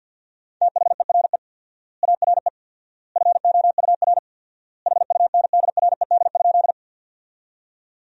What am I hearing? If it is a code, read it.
Morse code, heard as a keyed tone, about 49 wpm, 700 Hz. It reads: THERE ARE FOUR HUNDRED?
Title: 4x Word-Spacing - Sentences from Top 400 Words - 50wpm